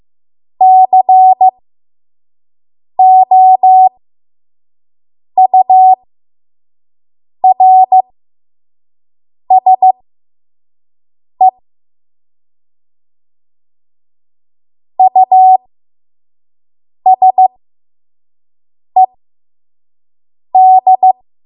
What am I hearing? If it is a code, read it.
Morse code, heard as a keyed tone, about 15 wpm, spaced as for 5 wpm, 750 Hz. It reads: COURSE USED
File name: Morse / 05WPM